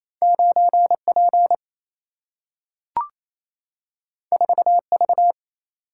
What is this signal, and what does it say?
Morse code, heard as a keyed tone, about 28 wpm, 700 Hz. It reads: 9 P 4 V
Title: Single-Letter Single-Number 28wpm